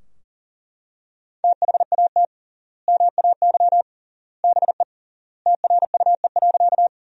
Morse code, heard as a keyed tone, about 40 wpm, 700 Hz. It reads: THAT MAY BE TRUE.
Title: 3x Word-Spacing - Sentences from Top 400 Words - 40wpm